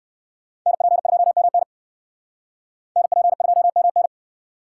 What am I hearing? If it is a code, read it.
Morse code, heard as a keyed tone, about 59 wpm, 700 Hz. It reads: NP2MN NP2MN